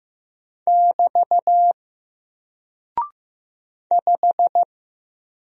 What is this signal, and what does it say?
Morse code, heard as a keyed tone, about 15 wpm, 700 Hz. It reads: <BT> 5
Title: Single Character (Letter:Number:Punctuation:Prosign) Alt 15wpm